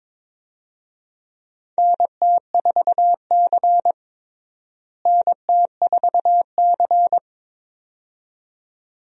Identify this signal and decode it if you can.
Morse code, heard as a keyed tone, about 22 wpm, 700 Hz. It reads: NT4C NT4C
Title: USA Call Signs - 4 Characters - Repeated 2x - 22wpm